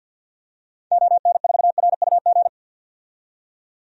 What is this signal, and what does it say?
Morse code, heard as a keyed tone, about 51 wpm, 700 Hz. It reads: ON4RUG
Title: DXCC-100 Call Signs - 6 Characters - 50wpm